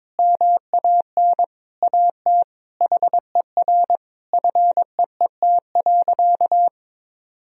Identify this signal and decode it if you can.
Morse code, heard as a keyed tone, about 22 wpm, 700 Hz. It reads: MAN AT HER FEET.
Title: Sentences from Top 1000 Words - Review Part 1 - 22wpm